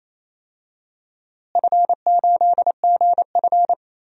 Morse code, heard as a keyed tone, about 28 wpm, 700 Hz. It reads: F8GF
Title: DXCC-100 Call Signs - 4 Characters - Repeated 3x - 28wpm